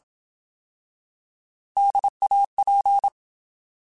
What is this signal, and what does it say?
Morse code, heard as a keyed tone, about 27 wpm, 800 Hz. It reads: DAP